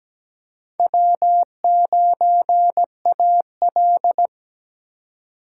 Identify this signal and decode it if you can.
Morse code, heard as a keyed tone, about 17 wpm, 700 Hz. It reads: W9AL
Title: USA Call Signs - 4x Character Spacing - 17wpm